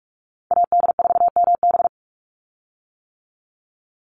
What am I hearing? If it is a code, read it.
Morse code, heard as a keyed tone, about 44 wpm, 700 Hz. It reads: AD4MB
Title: USA Call Signs 45wpm